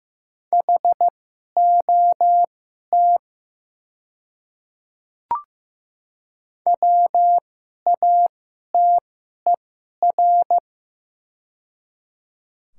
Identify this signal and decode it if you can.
Morse code, heard as a keyed tone, about 15 wpm, spaced as for 11 wpm, 700 Hz. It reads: HOT WATER
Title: 2x Character Spacing - Top 100 Words - 15wpm